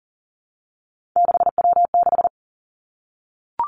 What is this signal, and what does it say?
Morse code, heard as a keyed tone, about 41 wpm, 700 Hz. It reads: 6W6
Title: Three Challenging Letter-Number Combinations 40wpm